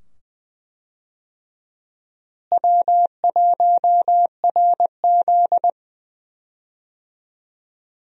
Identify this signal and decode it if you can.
Morse code, heard as a keyed tone, about 20 wpm, 700 Hz. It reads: W1RZ
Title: USA Call Signs - 4 Characters - Repeated 2x - 20wpm